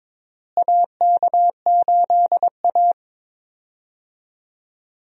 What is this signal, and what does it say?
Morse code, heard as a keyed tone, about 21 wpm, 700 Hz. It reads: AK8A
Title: USA Call Signs - 4 Characters - Repeated 3x - 22wpm